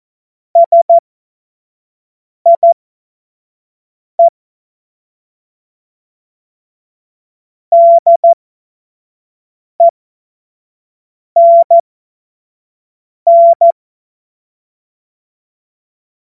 Morse code, heard as a keyed tone, about 14 wpm, spaced as for 5 wpm, 680 Hz. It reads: SIE DENN
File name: Morsekurs / Witze / Witz 26 März 2024 25 F70